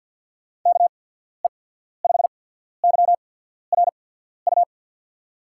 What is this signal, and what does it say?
Morse code, heard as a keyed tone, about 53 wpm, spaced as for 14 wpm, 700 Hz. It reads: KE5YRU
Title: USA Call Signs - 6 Characters - 8x Character Spacing - 50wpm